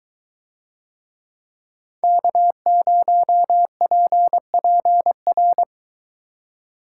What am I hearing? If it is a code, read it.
Morse code, heard as a keyed tone, about 21 wpm, 700 Hz. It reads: K0PPR